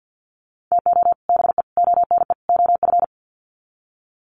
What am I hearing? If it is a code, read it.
Morse code, heard as a keyed tone, about 50 wpm, 700 Hz. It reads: TO BE ONE OF
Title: Sets of 4 Words 50wpm